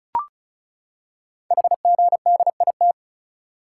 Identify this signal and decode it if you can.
Morse code, heard as a keyed tone, about 36 wpm, 700 Hz. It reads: HGDIT
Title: Code Groups - 5 Letters - 35wpm